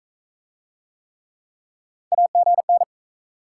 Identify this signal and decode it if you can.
Morse code, heard as a keyed tone, about 45 wpm, 700 Hz. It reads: AGN